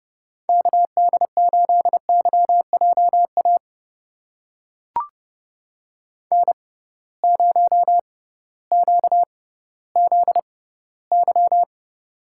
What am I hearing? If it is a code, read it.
Morse code, heard as a keyed tone, about 30 wpm, spaced as for 16 wpm, 700 Hz. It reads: KD8YJA N0QZY
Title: USA Call Signs - 6x Character Spacing - 30wpm